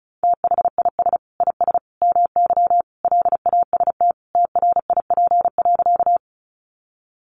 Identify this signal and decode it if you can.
Morse code, heard as a keyed tone, about 34 wpm, 700 Hz. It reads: THIS IS MY LAST TRIP.
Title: Sentences from Top 1300 Words 35wpm